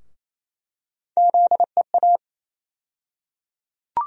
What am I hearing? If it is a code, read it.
Morse code, heard as a keyed tone, about 28 wpm, 700 Hz. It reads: ZEA
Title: USA Call Sign Postfixes - 3 Letters 28wpm